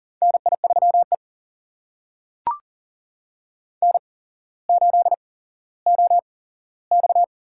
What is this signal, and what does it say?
Morse code, heard as a keyed tone, about 41 wpm, spaced as for 17 wpm, 700 Hz. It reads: NI3E N8OX
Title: USA Call Signs - 4 Characters - 8x Character Spacing - 40wpm